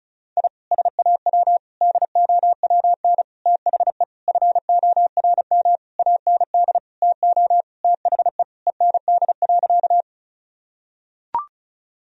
Morse code, heard as a keyed tone, about 34 wpm, 700 Hz. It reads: I SAW DOWN THE FORM AND TO THE END.
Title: Sentences from Top 1000 Words - Review Part 1 - 35wpm